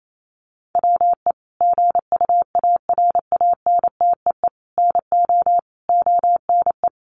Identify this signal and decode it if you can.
Morse code, heard as a keyed tone, about 28 wpm, 700 Hz. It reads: WE GUARANTEE NO ONE
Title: A Sound Of Thunder 28wpm